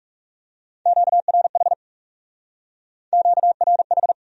Morse code, heard as a keyed tone, about 35 wpm, 700 Hz. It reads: QRH QRH
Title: Speed-Racing Q-Codes - 30wpm